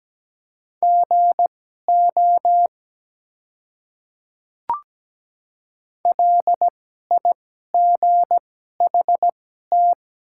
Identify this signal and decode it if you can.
Morse code, heard as a keyed tone, about 17 wpm, spaced as for 12 wpm, 700 Hz. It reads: GO LIGHT
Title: 2x Character Spacing - Top 200 Words - 17wpm